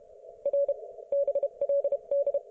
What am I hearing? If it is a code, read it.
Morse code, heard as a keyed tone, about 31 wpm, 560 Hz. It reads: R BLD